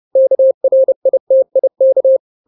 Morse code, heard as a keyed tone, about 30 wpm, 530 Hz. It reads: KRITIK
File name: daily / 29wpm